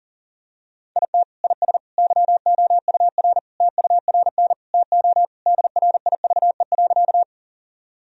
Morse code, heard as a keyed tone, about 40 wpm, 700 Hz. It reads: IT IS YOUR TURN TO DRIVE.
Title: Sentences from Top 1000 Words - Review Part 2 - 40wpm